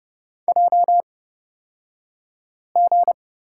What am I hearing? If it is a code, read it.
Morse code, heard as a keyed tone, about 30 wpm, spaced as for 5 wpm, 700 Hz. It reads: JG
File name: Lesson 38d - Letter q - Two Characters 30wpm